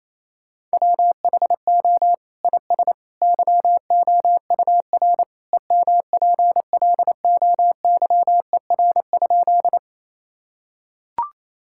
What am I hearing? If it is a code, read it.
Morse code, heard as a keyed tone, about 28 wpm, 700 Hz. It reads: WHO IS YOUR EMPLOYER?
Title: Sentences from Top 1600 Words 28wpm